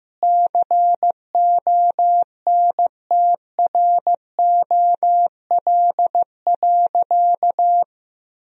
Morse code, heard as a keyed tone, about 15 wpm, 700 Hz. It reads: CONTROL.
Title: Sentences from Top 1000 Words - Review Part 3 - 15wpm